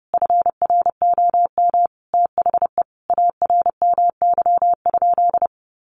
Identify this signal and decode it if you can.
Morse code, heard as a keyed tone, about 30 wpm, 700 Hz. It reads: FROM THE ARMY?
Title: Sentences from Top 800 Words 30wpm